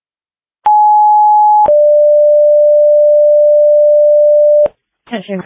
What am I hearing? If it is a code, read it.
Two-tone page, heard as a one-second tone, then a three-second tone, about 840 then 590 Hz. It is heard from 0.7 to 4.7 s.